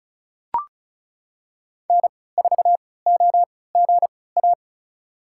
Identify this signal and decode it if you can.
Morse code, heard as a keyed tone, about 36 wpm, spaced as for 20 wpm, 700 Hz. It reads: N4OGA
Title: USA Call Signs - 5 Characters - 3x Character Spacing - 35wpm